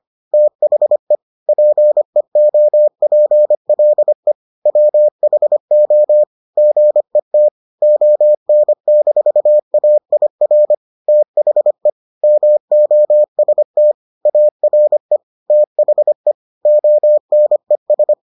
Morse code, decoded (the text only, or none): THE PEOPLE WHO GET ON-AIR THE MOST ARE THE ONES